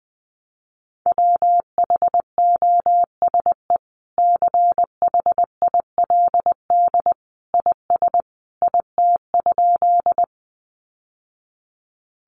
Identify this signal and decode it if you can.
Morse code, heard as a keyed tone, about 19 wpm, 700 Hz. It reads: WHOSE CHILD IS IT?
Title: Sentences from Top 1200 Words 20wpm